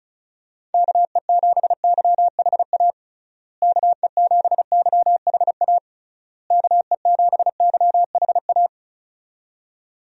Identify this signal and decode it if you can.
Morse code, heard as a keyed tone, about 36 wpm, 700 Hz. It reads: KE7YHA KE7YHA KE7YHA
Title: USA Call Signs - 6 Characters - Repeated 3x - 35wpm